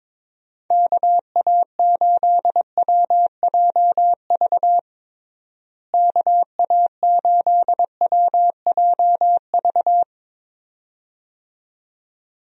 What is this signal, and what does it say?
Morse code, heard as a keyed tone, about 22 wpm, 700 Hz. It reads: KA8WJV KA8WJV
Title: USA Call Signs - 6 Characters - Repeated 2x - 22wpm